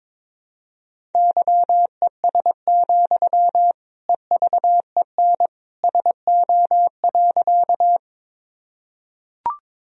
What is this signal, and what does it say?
Morse code, heard as a keyed tone, about 21 wpm, 700 Hz. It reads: YES, EVEN SO.
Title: Sentences from Top 1000 Words - Review Part 1 - 22wpm